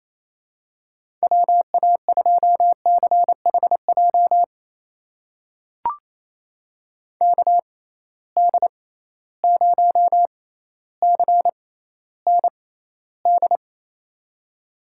Morse code, heard as a keyed tone, about 27 wpm, spaced as for 14 wpm, 700 Hz. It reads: WA2CHJ KD0CND
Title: USA Call Signs - 6 Characters - 6x Character Spacing - 28wpm